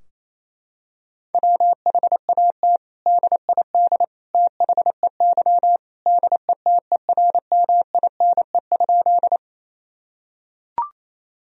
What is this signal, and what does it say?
Morse code, heard as a keyed tone, about 28 wpm, 700 Hz. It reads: WHAT DID THEY DETERMINE?
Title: Sentences from Top 1000 Words - Review Part 4 - 28wpm